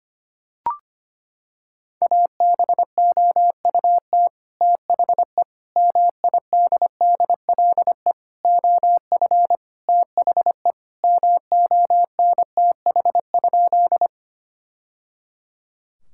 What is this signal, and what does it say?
Morse code, heard as a keyed tone, about 25 wpm, 700 Hz. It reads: ABOUT THE MIDDLE OF THE MONTH?
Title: Sentences from Top 1000 Words - Review Part 6 - 25wpm